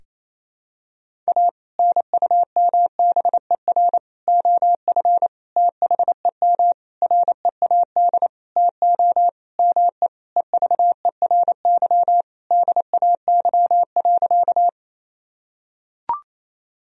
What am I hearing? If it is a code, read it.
Morse code, heard as a keyed tone, about 28 wpm, 700 Hz. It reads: A NUMBER OF THEM READ TO ME EVERY DAY.
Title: Sentences from Top 1000 Words - Review Part 1 - 28wpm